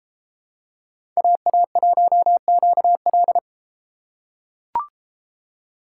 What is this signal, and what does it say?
Morse code, heard as a keyed tone, about 33 wpm, 700 Hz. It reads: AA1QL